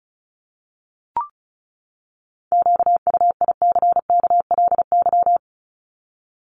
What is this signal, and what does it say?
Morse code, heard as a keyed tone, about 34 wpm, 700 Hz. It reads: QUICKLY
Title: LY Suffix - Top 100 Words with Base Words 35wpm